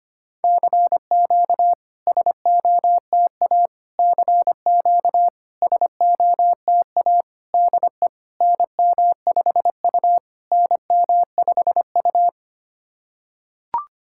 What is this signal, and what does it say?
Morse code, heard as a keyed tone, about 25 wpm, 700 Hz. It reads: CQ SOTA CQ SOTA DE NM5U NM5U
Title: SOTA QSO Practice 25wpm